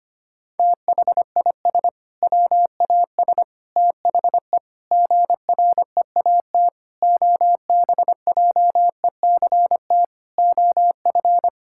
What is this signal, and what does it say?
Morse code, heard as a keyed tone, about 25 wpm, 700 Hz. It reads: THIS WAS THE GREAT OBJECT OF
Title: Sentences from Top 1000 Words - Review Part 5 - 25wpm